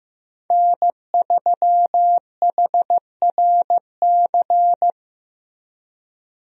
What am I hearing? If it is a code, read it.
Morse code, heard as a keyed tone, about 15 wpm, 700 Hz. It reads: N3HRC
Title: USA Call Signs - 5 Characters - Repeated 3x - 15wpm